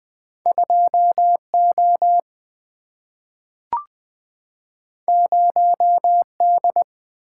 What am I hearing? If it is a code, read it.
Morse code, heard as a keyed tone, about 20 wpm, 700 Hz. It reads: 2 O 0 D
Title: Single-Letter Single-Number 20wpm